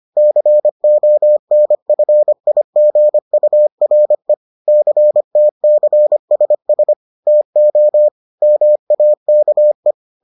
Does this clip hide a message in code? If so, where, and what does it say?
Yes: Morse code from 0.2 to 9.9 s, heard as a keyed tone, about 25 wpm, 600 Hz. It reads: CONFIGURE CTCSS TO MAKE